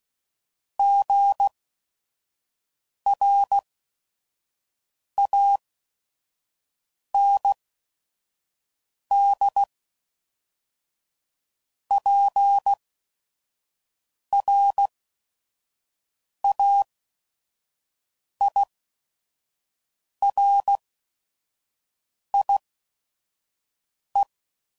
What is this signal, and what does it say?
Morse code, heard as a keyed tone, about 16 wpm, spaced as for 5 wpm, 800 Hz. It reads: GRANDPRAIRIE
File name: CW Code Groups @5WPM / fromSM's / Cities